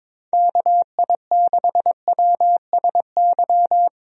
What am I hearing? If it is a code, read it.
Morse code, heard as a keyed tone, about 22 wpm, 700 Hz. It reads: KI6WSY
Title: USA Call Signs - 6 Characters - 22wpm